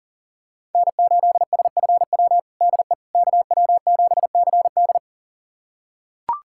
Morse code, heard as a keyed tone, about 40 wpm, 700 Hz. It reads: N8SFW DE KW7CD
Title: USA Callsign DE Callsign - 5 Characters - 40wpm